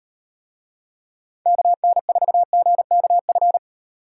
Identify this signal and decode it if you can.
Morse code, heard as a keyed tone, about 36 wpm, 700 Hz. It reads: KN4GKF